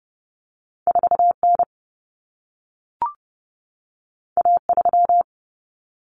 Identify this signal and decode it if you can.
Morse code, heard as a keyed tone, about 30 wpm, 700 Hz. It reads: 4N A3
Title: Common Cut Numbers - 2-Digit 30wpm